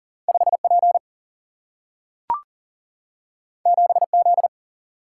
Morse code, heard as a keyed tone, about 40 wpm, 700 Hz. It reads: 5 P 7 Z
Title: Single-Letter Single-Number 40wpm